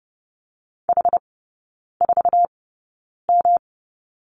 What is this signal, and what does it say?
Morse code, heard as a keyed tone, about 30 wpm, spaced as for 9 wpm, 700 Hz. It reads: H4M
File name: Lesson 19e - Letter w - Three Characters 30wpm